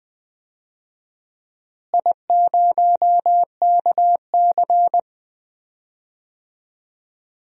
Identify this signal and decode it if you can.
Morse code, heard as a keyed tone, about 20 wpm, 700 Hz. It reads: I0KC
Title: DXCC-100 Call Signs - 4 Characters - Repeated 2x - 20wpm